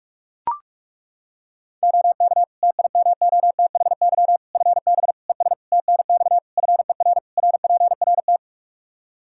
Morse code, heard as a keyed tone, about 41 wpm, 700 Hz. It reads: OK TIMOTHY FB ES TNX FER RPRT
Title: Practice QSO - Exchange 2 - Info - 45wpm